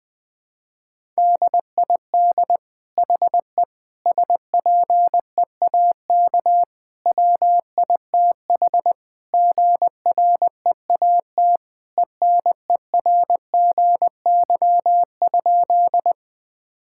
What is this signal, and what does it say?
Morse code, heard as a keyed tone, about 20 wpm, 700 Hz. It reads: DID HE SPEAK WITH GREAT ENERGY?
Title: Sentences from Top 1000 Words - Review Part 6 - 20wpm